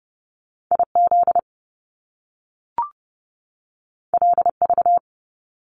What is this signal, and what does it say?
Morse code, heard as a keyed tone, about 30 wpm, 700 Hz. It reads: IZ LV
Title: Top Two Letter Combinations in Words 30wpm